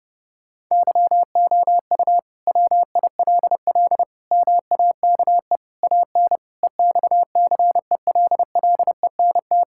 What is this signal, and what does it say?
Morse code, heard as a keyed tone, about 32 wpm, 700 Hz. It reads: YOU WILL MAKE AN EXCELLENT
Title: Sentences from Top 1000 Words - Review Part 5 - 30wpm